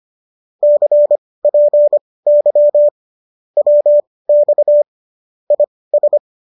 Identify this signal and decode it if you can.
Morse code, heard as a keyed tone, about 25 wpm, spaced as for 18 wpm, 590 Hz. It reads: CPY WX IS